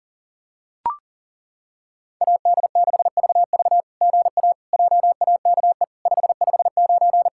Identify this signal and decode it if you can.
Morse code, heard as a keyed tone, about 40 wpm, 690 Hz. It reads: AD6VV GA JAKE 559
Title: SOTA QSO Practice 40wpm